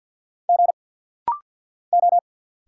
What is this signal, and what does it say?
Morse code, heard as a keyed tone, about 50 wpm, 690 Hz. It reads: G O
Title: Single Letter-Number Alt - Rapid-Fire 50wpm